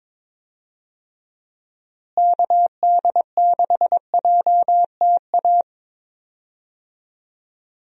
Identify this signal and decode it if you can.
Morse code, heard as a keyed tone, about 22 wpm, 700 Hz. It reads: KD6JTA
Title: USA Call Signs - 6 Characters - Repeated 3x - 22wpm